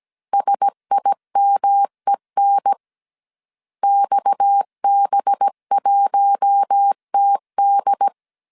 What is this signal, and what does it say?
Morse code, heard as a keyed tone, about 17 wpm, 800 Hz. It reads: SIMEN XB1TD
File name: CW / DET / 1 6 1500C S B 16wpm 2619